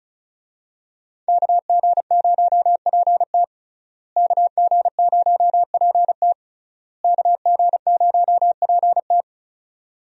Morse code, heard as a keyed tone, about 36 wpm, 700 Hz. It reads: KG0PT KG0PT KG0PT
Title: USA Call Signs - 5 Characters - Repeated 3x - 35wpm